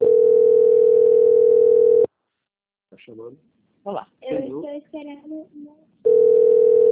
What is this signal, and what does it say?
Ringback tone (North America), 440 + 480 Hz, with speech.
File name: out- - - - - .38253